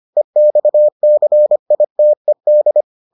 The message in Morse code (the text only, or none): EXCITED